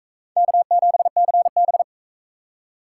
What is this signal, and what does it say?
Morse code, heard as a keyed tone, about 38 wpm, 700 Hz. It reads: K7CB